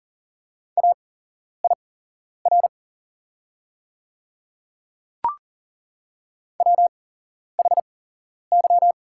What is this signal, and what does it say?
Morse code, heard as a keyed tone, about 40 wpm, spaced as for 11 wpm, 700 Hz. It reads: AIR WHY
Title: 8x Character Spacing - Top 200 Words - 40wpm